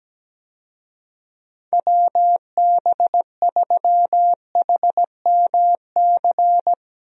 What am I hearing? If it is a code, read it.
Morse code, heard as a keyed tone, about 17 wpm, 700 Hz. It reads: WB3HMC
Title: USA Call Signs - 6 Characters - Repeated 2x - 17wpm